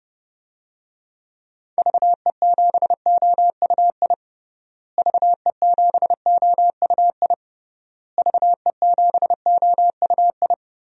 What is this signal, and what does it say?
Morse code, heard as a keyed tone, about 30 wpm, 700 Hz. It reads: VE7OUI VE7OUI VE7OUI
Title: DXCC-100 Call Signs - 6 Characters - Repeated 3x - 30wpm